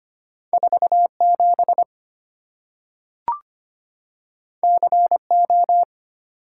Morse code, heard as a keyed tone, about 25 wpm, 690 Hz. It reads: 47 CO
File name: Two Letter-Number Combinations - 25wpm